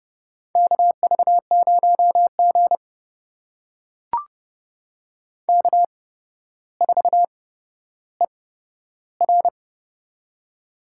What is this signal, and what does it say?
Morse code, heard as a keyed tone, about 32 wpm, spaced as for 13 wpm, 700 Hz. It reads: KV0G K4ER
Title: USA Call Signs - 4 Characters - 8x Character Spacing - 30wpm